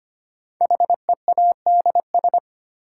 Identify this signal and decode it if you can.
Morse code, heard as a keyed tone, about 25 wpm, 700 Hz. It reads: HEADS
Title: Sentences from Top 1000 Words - Review Part 3 - New Word Drill - 25wpm